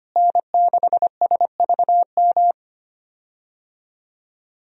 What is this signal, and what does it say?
Morse code, heard as a keyed tone, about 25 wpm, 700 Hz. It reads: N6SVM
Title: USA Call Signs - 5 Characters - Repeated 3x - 25wpm